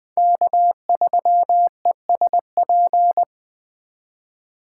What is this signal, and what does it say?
Morse code, heard as a keyed tone, about 20 wpm, 700 Hz. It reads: K3ESP
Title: Speed Racing - USA 1x3 Call Signs 17wpm